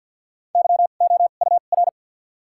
Morse code, heard as a keyed tone, about 54 wpm, 700 Hz. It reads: Y O U R
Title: 2x Character Spacing - Top 200 Words - 50wpm